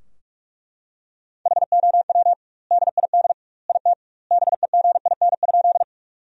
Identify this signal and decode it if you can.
Morse code, heard as a keyed tone, about 41 wpm, 700 Hz. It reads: HOW DID IT BEGIN?
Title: 2x Word-Spacing - Sentences from Top 300 Words - 45wpm